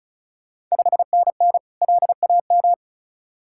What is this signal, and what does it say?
Morse code, heard as a keyed tone, about 35 wpm, 690 Hz. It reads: 5NN LAM
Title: Ontario QSO Party Exchange 35wpm